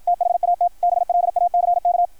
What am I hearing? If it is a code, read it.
Morse code, heard as a keyed tone, about 53 wpm, 690 Hz, with radio noise. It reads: THAT BLACK
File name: archived / CRSnet